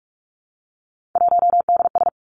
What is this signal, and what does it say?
Morse code, heard as a keyed tone, about 46 wpm, 700 Hz. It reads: 1DS
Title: Three Challenging Letter-Number Combinations 45wpm